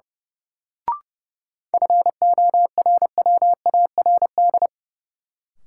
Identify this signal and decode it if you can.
Morse code, heard as a keyed tone, about 30 wpm, 700 Hz. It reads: FORWARD
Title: Sentences from Top 1300 Words - New Word Drill Alt - 30wpm